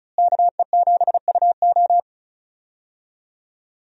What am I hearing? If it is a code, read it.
Morse code, heard as a keyed tone, about 36 wpm, 700 Hz. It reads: KE7UO
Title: USA Call Signs - 5 Characters - Repeated 3x - 35wpm